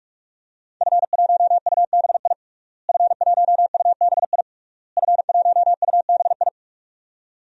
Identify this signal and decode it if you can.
Morse code, heard as a keyed tone, about 46 wpm, 700 Hz. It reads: F1UBI F1UBI F1UBI
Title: DXCC-100 Call Signs - 5 Characters - Repeated 3x - 45wpm